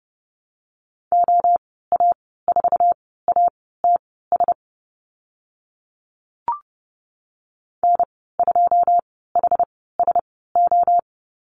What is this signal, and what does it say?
Morse code, heard as a keyed tone, about 30 wpm, spaced as for 17 wpm, 700 Hz. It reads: OA4ATS N2HSO
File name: Lesson 14g - Number 2 - Call Signs 30wpm